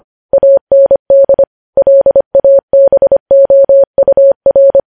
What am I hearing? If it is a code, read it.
Morse code, heard as a keyed tone, about 25 wpm, 550 Hz. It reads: AND LABOUR
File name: podcast / News headlines-1x-25x25